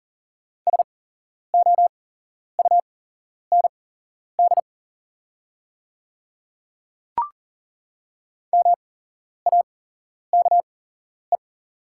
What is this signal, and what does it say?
Morse code, heard as a keyed tone, about 40 wpm, spaced as for 11 wpm, 700 Hz. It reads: SOUND MAKE
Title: 8x Character Spacing - Top 100 Words - 40wpm